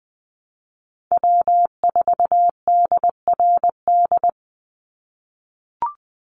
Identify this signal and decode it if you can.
Morse code, heard as a keyed tone, about 20 wpm, 700 Hz. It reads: W4DRD
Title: USA Call Signs 20wpm